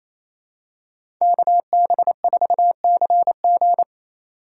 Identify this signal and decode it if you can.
Morse code, heard as a keyed tone, about 28 wpm, 700 Hz. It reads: KB4CG
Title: K1USN SST QSO Practice - 28wpm